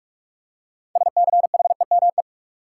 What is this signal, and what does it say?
Morse code, heard as a keyed tone, about 47 wpm, 700 Hz. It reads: SCHEME
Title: Six Letter Words - 45wpm